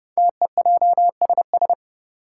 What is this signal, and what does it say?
Morse code, heard as a keyed tone, about 30 wpm, 700 Hz. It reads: TEJSS
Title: Code Groups - 5 Letters - 30wpm